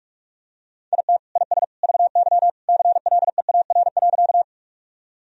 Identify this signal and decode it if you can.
Morse code, heard as a keyed tone, about 45 wpm, 700 Hz. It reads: IT IS VY CLEAR.
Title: Sentences from Top 500 Words and Abbreviations 45wpm